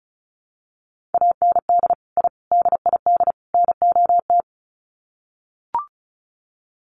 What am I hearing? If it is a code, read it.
Morse code, heard as a keyed tone, about 34 wpm, 700 Hz. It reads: AND I DID NOT
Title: Sets of 4 Words 35wpm